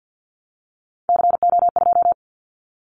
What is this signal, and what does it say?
Morse code, heard as a keyed tone, about 50 wpm, 700 Hz. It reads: CO2
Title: Callsign Prefixes for Top 100 DXCC - 50wpm